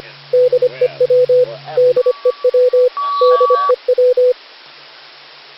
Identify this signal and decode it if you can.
Morse code, heard as a keyed tone, about 25 wpm, 500 Hz, with radio noise. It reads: DEW DEW DEW
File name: cw / Session5 cw training